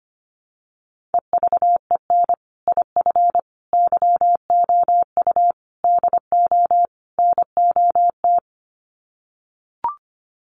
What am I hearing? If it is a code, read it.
Morse code, heard as a keyed tone, about 25 wpm, 700 Hz. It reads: EVEN IF YOU DO NOT
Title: Sets of 5 Words Encore 25wpm